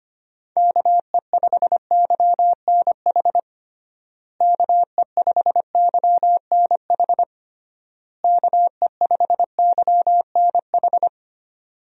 Morse code, heard as a keyed tone, about 25 wpm, 700 Hz. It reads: KE5YNH KE5YNH KE5YNH
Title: USA Call Signs - 6 Characters - Repeated 3x - 25wpm